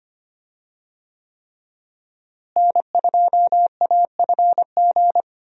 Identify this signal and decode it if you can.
Morse code, heard as a keyed tone, about 25 wpm, 700 Hz. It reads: N2AFG
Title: USA Call Signs - 5 Characters - Repeated 3x - 25wpm